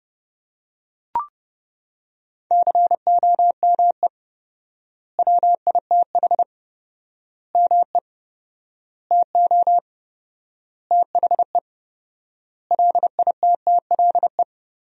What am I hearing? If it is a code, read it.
Morse code, heard as a keyed tone, about 30 wpm, 700 Hz. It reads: COME WITH ME TO THE LITTLE
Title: 4x Word-Spacing - Sentences from Top 200 Words - 30wpm